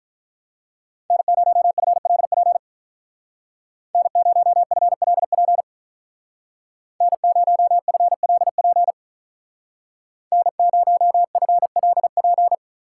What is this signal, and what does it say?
Morse code, heard as a keyed tone, about 44 wpm, 700 Hz. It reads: N0FLP N0FLP N0FLP N0FLP